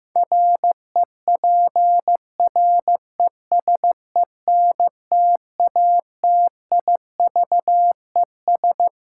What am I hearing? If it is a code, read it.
Morse code, heard as a keyed tone, about 15 wpm, 700 Hz. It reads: REPRESENTATIVES
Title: Sets of 5 Words Encore - New Word Drill - 15wpm